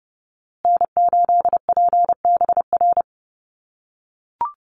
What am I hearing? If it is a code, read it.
Morse code, heard as a keyed tone, about 30 wpm, 700 Hz. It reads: N8PBR
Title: USA Call Signs Repeated 2x 30wpm